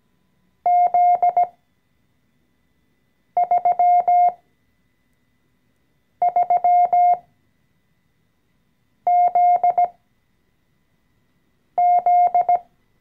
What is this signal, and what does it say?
Morse code, heard as a keyed tone, about 17 wpm, spaced as for 4 wpm, 710 Hz. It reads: Z33ZZ
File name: CW-Online-Kurs / Anfänger / Lektion 014